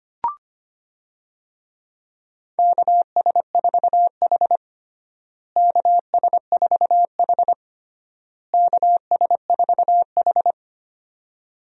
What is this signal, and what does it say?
Morse code, heard as a keyed tone, about 25 wpm, 700 Hz. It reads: KS4H KS4H KS4H
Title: USA Call Signs - 4 Characters - Repeated 3x - 25wpm